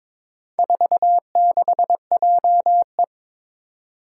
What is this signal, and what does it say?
Morse code, heard as a keyed tone, about 22 wpm, 700 Hz. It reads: 46JE
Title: Four Challenging Letter-Number Combinations 22wpm